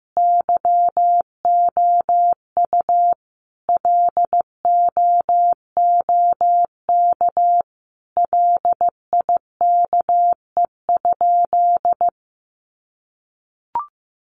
Morse code, heard as a keyed tone, about 15 wpm, 700 Hz. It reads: YOU LOOK LIKE?
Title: Sentences from Top 100 Words 15wpm